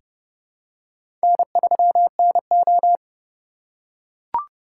Morse code, heard as a keyed tone, about 30 wpm, 700 Hz. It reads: N3NO
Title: USA Call Signs - 4 Characters - 3x Character Spacing - 30wpm